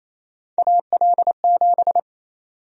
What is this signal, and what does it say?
Morse code, heard as a keyed tone, about 28 wpm, 700 Hz. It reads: AL7
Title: USA Call Sign Prefixes - 28wpm